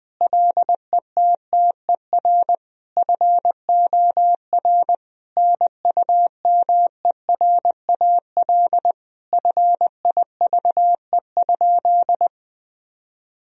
Morse code, heard as a keyed tone, about 21 wpm, 700 Hz. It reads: LETTER FOR NUMERAL FIVE?
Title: Sentences from Top 1000 Words - Review Part 2 - 20wpm